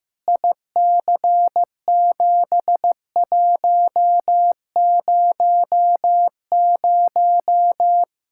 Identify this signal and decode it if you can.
Morse code, heard as a keyed tone, about 15 wpm, 700 Hz. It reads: IC7100
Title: Practice QSO - Exchange 2 - Info - 15wpm